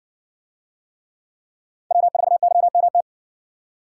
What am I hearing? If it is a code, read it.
Morse code, heard as a keyed tone, about 60 wpm, 700 Hz. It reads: W4YMT